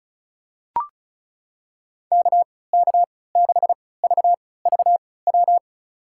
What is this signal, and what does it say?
Morse code, heard as a keyed tone, about 36 wpm, spaced as for 20 wpm, 700 Hz. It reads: KK6VVW
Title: USA Call Signs - 3x Character Spacing - 35wpm